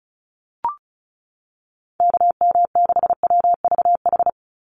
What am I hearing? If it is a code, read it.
Morse code, heard as a keyed tone, about 34 wpm, 700 Hz. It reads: KM6WVH
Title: USA Call Signs 35wpm